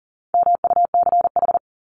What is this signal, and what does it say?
Morse code, heard as a keyed tone, about 40 wpm, 700 Hz. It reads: MUCH
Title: Top 300 Words 40wpm